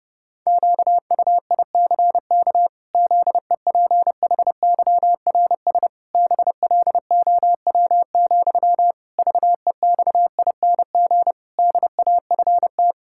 Morse code, heard as a keyed tone, about 30 wpm, 700 Hz. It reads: QUICK ZEPHYRS BLOW, VEXING DAFT
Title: Pangram Challenge v2 - 30wpm